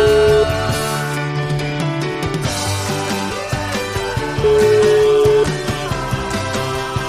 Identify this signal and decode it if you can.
Ringback tone (Europe, 425 Hz), with noise.